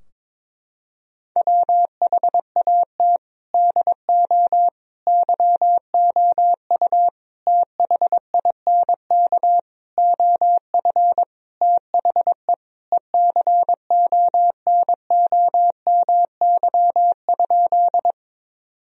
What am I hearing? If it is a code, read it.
Morse code, heard as a keyed tone, about 22 wpm, 700 Hz. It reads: WHAT DO YOU THINK OF THE ECONOMY?
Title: Sentences from Top 1000 Words - Review Part 3 - 22wpm